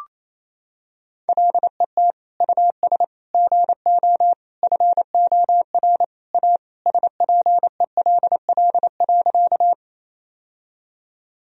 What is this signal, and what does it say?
Morse code, heard as a keyed tone, about 28 wpm, 700 Hz. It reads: LET US GO FOR A SPELL.
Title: Sentences from Top 1000 Words - Review Part 1 - 28wpm